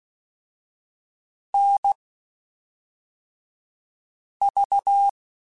Morse code, heard as a keyed tone, about 16 wpm, spaced as for 3 wpm, 800 Hz. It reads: NV
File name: (spaced) / QSOs Spaced-mp3s / WD0FBT 1 @5 sp